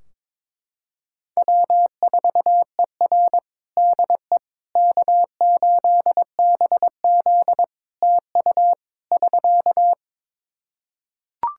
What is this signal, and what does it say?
Morse code, heard as a keyed tone, about 21 wpm, 700 Hz. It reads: W4ER DE K8BZ TU <SK>
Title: Practice QSO - Exchange 4 - Ending - 22wpm